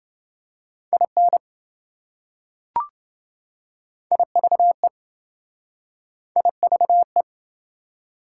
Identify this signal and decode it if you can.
Morse code, heard as a keyed tone, about 27 wpm, 700 Hz. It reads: IN IVE IVE